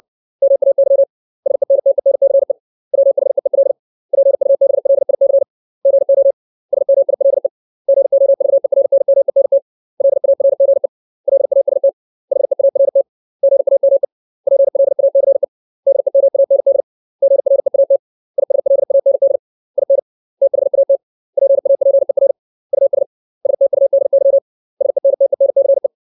Morse code, heard as a keyed tone, about 60 wpm, 550 Hz. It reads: MAY SENTENCE WHEEL PUBLIC GO SMILE GOVERNMENT LARGE LAST HEART GAME PLACE DEMAND GREAT ISLAND IN THAT PAPER US STUDY SENTENCE
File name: 632 words / 632 Words 60WPM